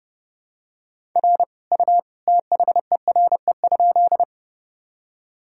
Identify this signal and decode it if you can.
Morse code, heard as a keyed tone, about 30 wpm, 700 Hz. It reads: R U THERE?
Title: Sentences from Top 100 Words and Abbreviations 30wpm